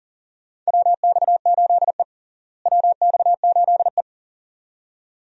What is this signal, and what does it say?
Morse code, heard as a keyed tone, about 40 wpm, 700 Hz. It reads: WX8E WX8E
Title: USA Call Signs - 4 Characters - Repeated 2x - 40wpm